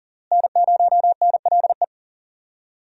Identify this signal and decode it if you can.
Morse code, heard as a keyed tone, about 40 wpm, 700 Hz. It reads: N0NLE
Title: USA Call Signs - 5 Characters - Repeated 3x - 40wpm